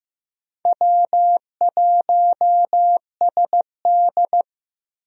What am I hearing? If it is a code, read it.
Morse code, heard as a keyed tone, about 15 wpm, 700 Hz. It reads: W1SD
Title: USA Call Signs - 4 Characters - 15wpm